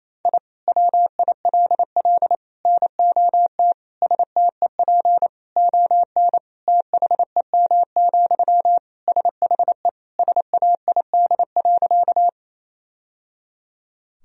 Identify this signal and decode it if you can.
Morse code, heard as a keyed tone, about 28 wpm, 700 Hz. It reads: I WILL NOT STEP ON THEM, SHE SAID.
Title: Sentences from Top 1000 Words - Review Part 2 - 28wpm